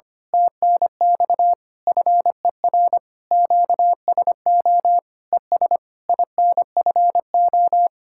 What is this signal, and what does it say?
Morse code, heard as a keyed tone, about 25 wpm, 700 Hz. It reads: TNX FER QSO ES INFO
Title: Speed-Racing Common QSO Phrases Encore - 25wpm